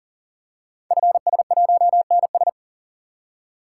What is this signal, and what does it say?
Morse code, heard as a keyed tone, about 41 wpm, 700 Hz. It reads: FS1NS
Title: Code Groups - 5 Letter:Numbers - 40wpm